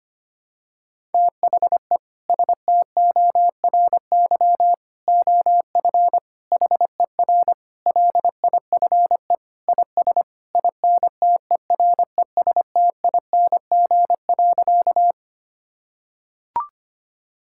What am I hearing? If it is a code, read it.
Morse code, heard as a keyed tone, about 25 wpm, 700 Hz. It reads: THE STORY OF HER LIFE IS INTERESTING.
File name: practice / 25wpm / Sentences with 3-Word Phrases - 25wpm